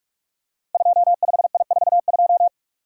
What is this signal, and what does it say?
Morse code, heard as a keyed tone, about 44 wpm, 700 Hz. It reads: 25I42
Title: Code Groups - 5 Letter:Numbers - 45wpm